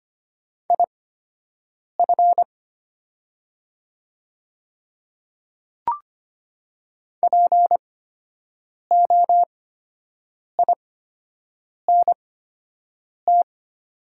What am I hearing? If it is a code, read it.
Morse code, heard as a keyed tone, about 25 wpm, spaced as for 7 wpm, 700 Hz. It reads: IF POINT
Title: 8x Character Spacing - Top 200 Words - 25wpm